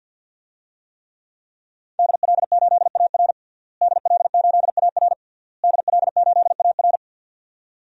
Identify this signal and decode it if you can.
Morse code, heard as a keyed tone, about 49 wpm, 700 Hz. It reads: DL8AR DL8AR DL8AR
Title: DXCC-100 Call Signs - 5 Characters - Repeated 3x - 50wpm